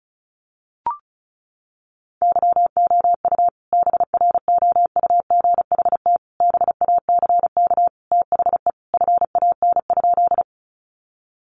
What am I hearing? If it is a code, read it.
Morse code, heard as a keyed tone, about 34 wpm, 700 Hz. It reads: YOU BROUGHT BACK THE FAN?
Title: Sentences from Top 1300 Words 35wpm